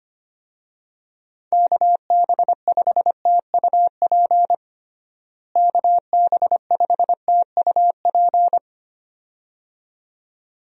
Morse code, heard as a keyed tone, about 25 wpm, 700 Hz. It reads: KB5TUP KB5TUP
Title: USA Call Signs - 6 Characters - Repeated 2x - 25wpm